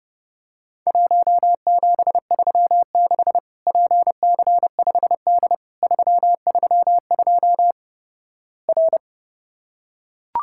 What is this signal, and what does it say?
Morse code, heard as a keyed tone, about 32 wpm, 700 Hz. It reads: 1736 PC5D 332 R
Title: Worked All Europe DX Contest - QTC Exchange - 30wpm